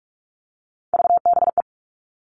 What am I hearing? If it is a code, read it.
Morse code, heard as a keyed tone, about 44 wpm, 700 Hz. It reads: VBE
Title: All Cut Numbers - 3-Digit 45wpm